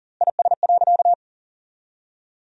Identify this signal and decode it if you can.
Morse code, heard as a keyed tone, about 40 wpm, 700 Hz. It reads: IS.
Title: 2x Word-Spacing - Sentences from Top 300 Words - 40wpm